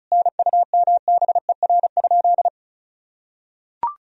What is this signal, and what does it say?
Morse code, heard as a keyed tone, about 34 wpm, 700 Hz. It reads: NUMBER?
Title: Sentences from Top 1000 Words - Review Part 1 - 35wpm